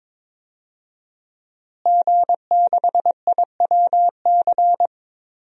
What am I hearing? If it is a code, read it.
Morse code, heard as a keyed tone, about 22 wpm, 700 Hz. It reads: G6IWC
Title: DXCC-100 Call Signs - 5 Characters - Repeated 3x - 22wpm